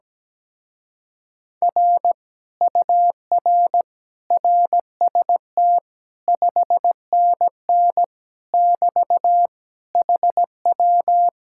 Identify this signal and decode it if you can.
Morse code, heard as a keyed tone, about 17 wpm, 700 Hz. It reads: R UR RST 5NN <BT> HW
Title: Speed-Racing Common QSO Phrases - 17wpm